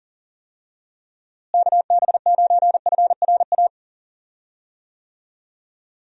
Morse code, heard as a keyed tone, about 40 wpm, 700 Hz. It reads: KB9FRA
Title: USA Call Signs - 6 Characters - Repeated 3x - 40wpm